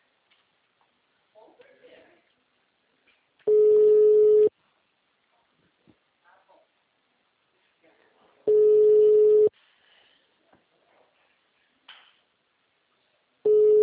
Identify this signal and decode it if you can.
Ringback tone (Europe, 425 Hz), with speech.
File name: out- -2004- - - .14681